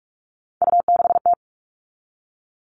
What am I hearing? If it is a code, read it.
Morse code, heard as a keyed tone, about 44 wpm, 700 Hz. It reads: U6T
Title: All Cut Numbers - 3-Digit 45wpm